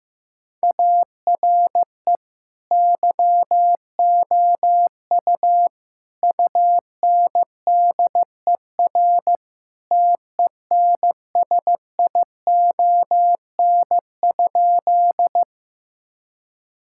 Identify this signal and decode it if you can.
Morse code, heard as a keyed tone, about 16 wpm, 700 Hz. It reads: ARE YOU UNDER TENSION?
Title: Sentences from Top 1600 Words 15wpm